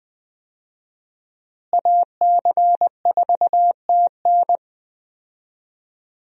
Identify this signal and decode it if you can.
Morse code, heard as a keyed tone, about 20 wpm, 700 Hz. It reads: AC4TN
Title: USA Call Signs - 5 Characters - Repeated 3x - 20wpm